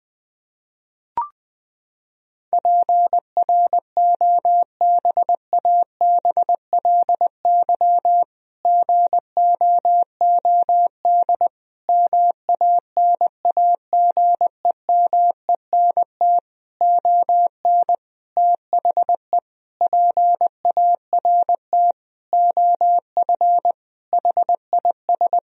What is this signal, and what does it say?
Morse code, heard as a keyed tone, about 20 wpm, 700 Hz. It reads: PROBABLY GOOD MANAGEMENT ON THE PART OF HIS
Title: Sentences from Top 1000 Words - Review Part 4 - 20wpm